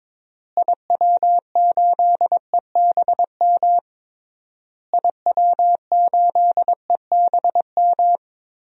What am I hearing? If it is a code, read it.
Morse code, heard as a keyed tone, about 22 wpm, 700 Hz. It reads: IW8EBM IW8EBM
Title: DXCC-100 Call Signs - 6 Characters - Repeated 3x - 22wpm